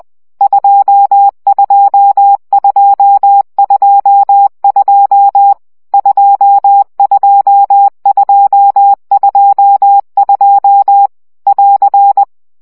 Morse code, heard as a keyed tone, about 20 wpm, 800 Hz. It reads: 22222 22222 <AR>
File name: CW gyakorlas 2 17 baud